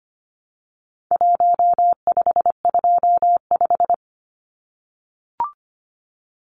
Morse code, heard as a keyed tone, about 25 wpm, 700 Hz. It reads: 1525
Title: Numbers - Contest Style - 4-Digit 25wpm